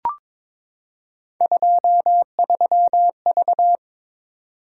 Morse code, heard as a keyed tone, about 23 wpm, 700 Hz. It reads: 23V
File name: practice / 22wpm / Three Letter-Number Combinations - 22wpm